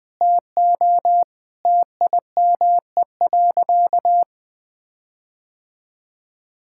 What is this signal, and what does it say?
Morse code, heard as a keyed tone, about 20 wpm, 700 Hz. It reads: TO TIME.
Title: Sentences with 4-Word Phrases - Spoken Beginning - 20wpm